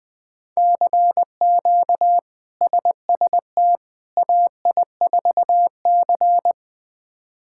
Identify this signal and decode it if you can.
Morse code, heard as a keyed tone, about 21 wpm, 700 Hz. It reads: CQ SST AI4C
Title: K1USN SST QSO Practice - 20wpm